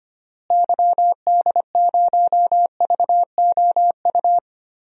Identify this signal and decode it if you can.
Morse code, heard as a keyed tone, about 25 wpm, 700 Hz. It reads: YD0VOU
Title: DXCC-100 Call Signs - 6 Characters - Repeated 3x - 25wpm